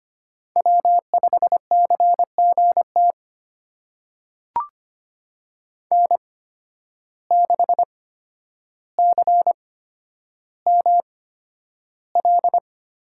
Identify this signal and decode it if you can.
Morse code, heard as a keyed tone, about 25 wpm, spaced as for 11 wpm, 700 Hz. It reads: W5CGT N6CML
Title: USA Call Signs - 5 Characters - 8x Character Spacing - 25wpm